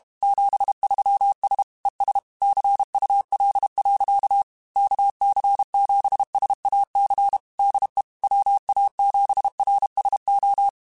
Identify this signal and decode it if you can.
Morse code, heard as a keyed tone, about 32 wpm, 800 Hz. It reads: 73S ES CUL. KC7SAC DE WA7RSO
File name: QSOs / QSOs-mp3s / kc7sac@30